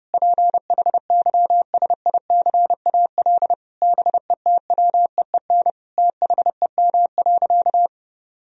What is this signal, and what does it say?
Morse code, heard as a keyed tone, about 32 wpm, 700 Hz. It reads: PHYSICAL BETWEEN THEM.
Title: Sentences from Top 1000 Words - Review Part 4 - 30wpm